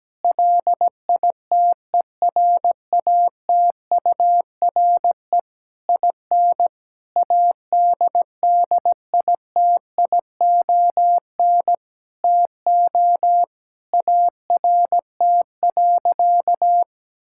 Morse code, heard as a keyed tone, about 17 wpm, 700 Hz. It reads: LITERATURE IN ADDITION TO ART.
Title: Sentences from Top 1000 Words - Review Part 6 - 17wpm